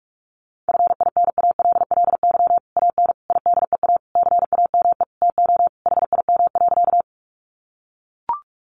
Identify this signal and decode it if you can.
Morse code, heard as a keyed tone, about 44 wpm, 700 Hz. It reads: FINALLY AN IDEA CAME TO HIM.
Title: Sentences from Top 1000 Words 45wpm